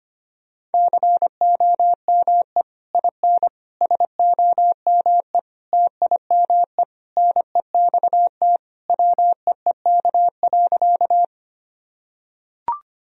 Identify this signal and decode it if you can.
Morse code, heard as a keyed tone, about 25 wpm, 700 Hz. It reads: COME IN SOME TIME NEXT WEEK.
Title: Sentences from Top 1000 Words - Review Part 2 - 25wpm